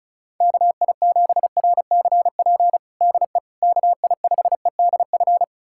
Morse code, heard as a keyed tone, about 36 wpm, 700 Hz. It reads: KI7RCP DE KI5EDF
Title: USA Callsign DE Callsign 35wpm